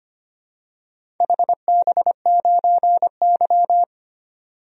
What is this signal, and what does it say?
Morse code, heard as a keyed tone, about 25 wpm, 700 Hz. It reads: HB9Y
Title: DXCC-100 Call Signs - 4 Characters - 25wpm